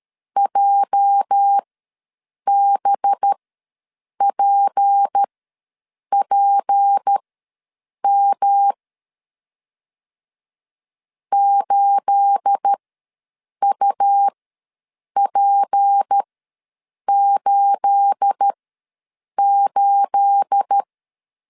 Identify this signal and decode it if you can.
Morse code, heard as a keyed tone, about 13 wpm, spaced as for 7 wpm, 800 Hz. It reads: JBPPM 8UP88
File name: LEZIONI / 9 28 ws 12wpm 3354